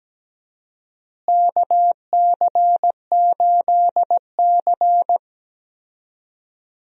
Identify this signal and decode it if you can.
Morse code, heard as a keyed tone, about 17 wpm, 700 Hz. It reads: KC8C
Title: USA Call Signs - 4 Characters - Repeated 3x - 17wpm